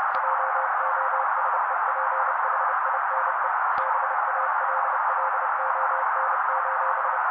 Morse code, heard as a keyed tone, about 30 wpm, 530 Hz. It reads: JOSEPHINE BALLON:
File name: daily / 29wpm noise